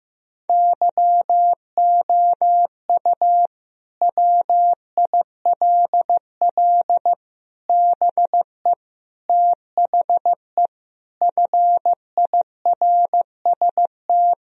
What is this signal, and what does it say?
Morse code, heard as a keyed tone, about 16 wpm, 700 Hz. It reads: YOU WILL BE THE FIRST
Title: Sentences from Top 1000 Words - Review Part 4 - 15wpm